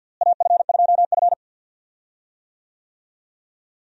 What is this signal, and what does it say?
Morse code, heard as a keyed tone, about 50 wpm, 700 Hz. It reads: AF2F
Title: USA Call Signs - 4 Characters - Repeated 3x - 50wpm